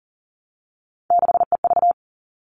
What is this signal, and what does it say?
Morse code, heard as a keyed tone, about 40 wpm, 700 Hz. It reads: 6EV
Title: All Cut Numbers - 3-Digit 40wpm